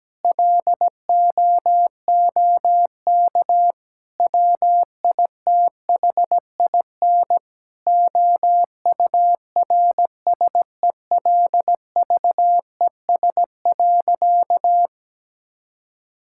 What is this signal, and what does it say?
Morse code, heard as a keyed tone, about 17 wpm, 700 Hz. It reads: LOOK WITHIN OURSELVES.
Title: Sentences from Top 1600 Words 17wpm